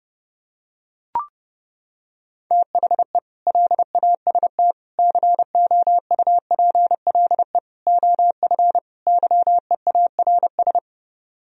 Morse code, heard as a keyed tone, about 28 wpm, 700 Hz. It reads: THE LAST COUPLE OF YEARS